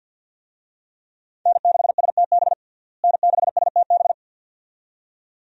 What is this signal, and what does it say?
Morse code, heard as a keyed tone, about 52 wpm, 700 Hz. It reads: N6STB N6STB
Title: USA Call Signs - 5 Characters - Repeated 2x - 50wpm